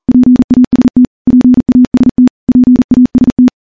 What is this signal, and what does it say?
Morse code, heard as a keyed tone, about 40 wpm, 250 Hz. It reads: PAST PAST PAST
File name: morse-training-40WPM